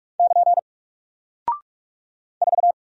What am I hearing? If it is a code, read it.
Morse code, heard as a keyed tone, about 44 wpm, 700 Hz. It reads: <KN> 4
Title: Single Character (Letter:Number:Punctuation:Prosign) 45wpm